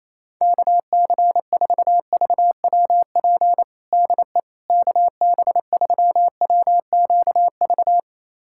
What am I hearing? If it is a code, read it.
Morse code, heard as a keyed tone, about 28 wpm, 700 Hz. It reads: KC4VWP DE KB3WQV
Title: USA Callsign DE Callsign 28wpm